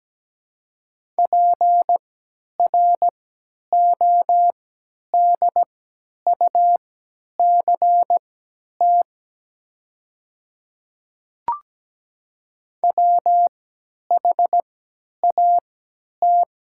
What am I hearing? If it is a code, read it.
Morse code, heard as a keyed tone, about 17 wpm, spaced as for 10 wpm, 700 Hz. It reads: PRODUCT WHAT
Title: 3x Character Spacing - Top 500 Words - 17wpm